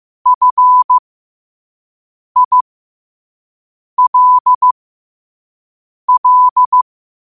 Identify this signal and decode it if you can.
Morse code, heard as a keyed tone, about 15 wpm, spaced as for 5 wpm, 1000 Hz. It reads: FILL
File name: qotdfiles / 5wpm